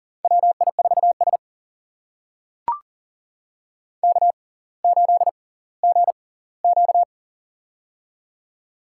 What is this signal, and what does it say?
Morse code, heard as a keyed tone, about 39 wpm, spaced as for 20 wpm, 700 Hz. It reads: WI4S K8GQ
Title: USA Call Signs - 4 Characters - 6x Character Spacing - 40wpm